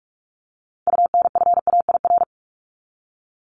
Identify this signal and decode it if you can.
Morse code, heard as a keyed tone, about 44 wpm, 700 Hz. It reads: UNFAIR
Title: Top 1751-2000 Words 45wpm